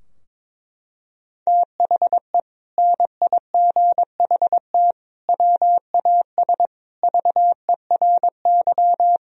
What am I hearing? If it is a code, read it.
Morse code, heard as a keyed tone, about 22 wpm, 700 Hz. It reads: THE NIGHT WAS VERY
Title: Sentences from Top 1000 Words - Review Part 2 - 22wpm